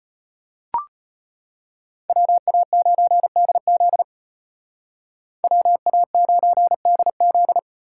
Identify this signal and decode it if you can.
Morse code, heard as a keyed tone, about 33 wpm, 700 Hz. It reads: WA9DZ WA9DZ